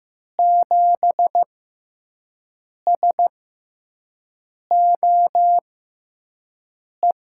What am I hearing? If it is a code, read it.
Morse code, heard as a keyed tone, about 15 wpm, spaced as for 5 wpm, 700 Hz. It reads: 7SOE
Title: USA Call Signs - 6 Characters - 6x Character Spacing - 15wpm